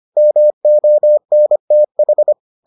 Morse code, heard as a keyed tone, about 25 wpm, 600 Hz. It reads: MONTH